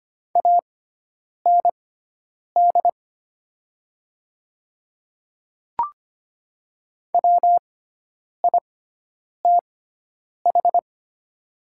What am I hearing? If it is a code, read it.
Morse code, heard as a keyed tone, about 25 wpm, spaced as for 9 wpm, 700 Hz. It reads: AND WITH
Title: 6x Character Spacing - Top 100 Words - 25wpm